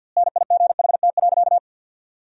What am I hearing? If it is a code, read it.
Morse code, heard as a keyed tone, about 51 wpm, 700 Hz. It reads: NIGHT.
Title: 2x Word-Spacing - Sentences from Top 300 Words - 50wpm